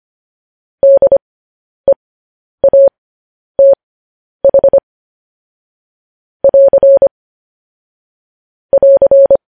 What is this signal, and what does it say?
Morse code, heard as a keyed tone, about 25 wpm, spaced as for 10 wpm, 550 Hz. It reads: DEATH <AR> <AR>
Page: News Headlines in Morse Code at 10 WPM